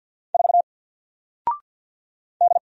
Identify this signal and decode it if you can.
Morse code, heard as a keyed tone, about 48 wpm, 700 Hz. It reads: 4 D
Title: Single Character (Letter:Number:Punctuation:Prosign) Alt 50wpm